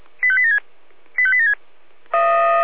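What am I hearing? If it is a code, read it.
Five-tone selective call (ZVEI), heard as quick runs of five tones, tones of about 80 ms. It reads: 85876 85876
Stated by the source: ZVEI five-tone call received on SDR.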